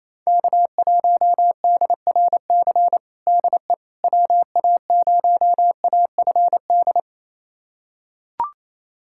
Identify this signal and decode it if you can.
Morse code, heard as a keyed tone, about 28 wpm, 700 Hz. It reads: K1DRC DE WA0AFD
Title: USA Callsign DE Callsign 28wpm